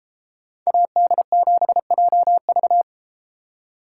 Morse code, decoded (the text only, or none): AD7JV